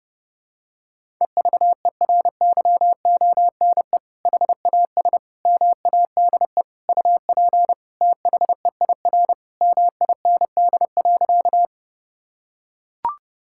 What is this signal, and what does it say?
Morse code, heard as a keyed tone, about 32 wpm, 700 Hz. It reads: EVERYONE HAS MADE UP THEIR MIND.
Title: Sentences from Top 1000 Words - Review Part 5 - 30wpm